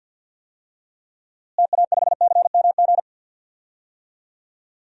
Morse code, heard as a keyed tone, about 49 wpm, 700 Hz. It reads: TA5CMG
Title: DXCC-100 Call Signs - 6 Characters - Repeated 2x - 50wpm